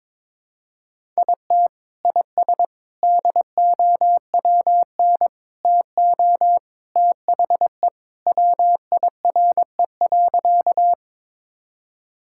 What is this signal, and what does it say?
Morse code, heard as a keyed tone, about 22 wpm, 700 Hz. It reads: IT IS DOWN TO THE WIRE.
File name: practice / 22wpm / Sentences with 3-Word Phrases - 22wpm